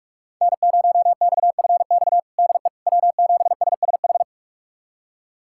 Morse code, heard as a keyed tone, about 42 wpm, 700 Hz. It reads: N0XFX DE W7SSH
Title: USA Callsign DE Callsign - 5 Characters - 45wpm